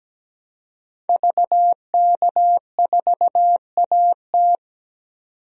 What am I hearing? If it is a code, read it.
Morse code, heard as a keyed tone, about 17 wpm, 700 Hz. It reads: VK4AT
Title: DXCC-100 Call Signs - 5 Characters - 17wpm